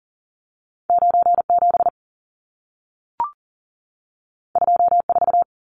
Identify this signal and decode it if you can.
Morse code, heard as a keyed tone, about 40 wpm, 700 Hz. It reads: 97 24
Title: Numbers - 2-Digit 40wpm